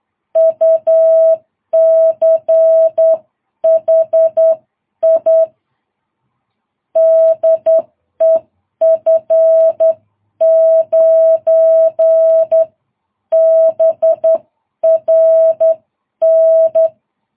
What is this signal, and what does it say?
Morse code, heard as a keyed tone, about 10 wpm, spaced as for 9 wpm, 660 Hz. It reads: UCHI DE F9BRN
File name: jh8cbh / 2022cw